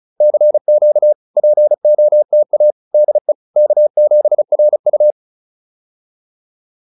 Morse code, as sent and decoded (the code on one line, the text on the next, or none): -.-. --.- / .--. --- - .- / -.. / -.- --... .-. ..-
CQ POTA DE K7RU